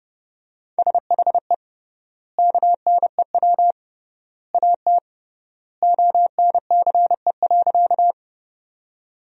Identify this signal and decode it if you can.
Morse code, heard as a keyed tone, about 32 wpm, 700 Hz. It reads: SHE KNEW AT ONCE.
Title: 3x Word-Spacing - Sentences from Top 400 Words - 30wpm